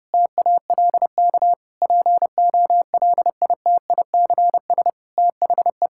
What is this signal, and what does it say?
Morse code, heard as a keyed tone, about 32 wpm, 700 Hz. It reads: TALK POLITICS THE
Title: Sentences from Top 1000 Words - Review Part 6 - 30wpm